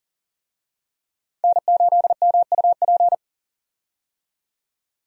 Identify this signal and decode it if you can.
Morse code, heard as a keyed tone, about 38 wpm, 700 Hz. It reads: N8MUP
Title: USA Call Signs - 5 Characters - 40wpm